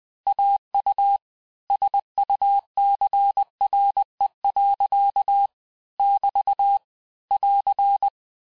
Morse code, heard as a keyed tone, about 20 wpm, 800 Hz. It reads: AU SUCRE. <BT> <AR>
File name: cours-du-mercredi / audios / cours-858-texte-clais-vitesse-19mpm